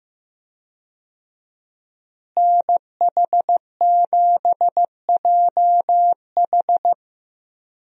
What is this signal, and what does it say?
Morse code, heard as a keyed tone, about 15 wpm, 700 Hz. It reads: NH7JH
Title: USA Call Signs - 5 Characters - Repeated 2x - 15wpm